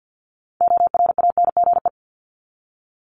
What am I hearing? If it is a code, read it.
Morse code, heard as a keyed tone, about 50 wpm, 700 Hz. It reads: ORANGE
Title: Top 1751-2000 Words 50wpm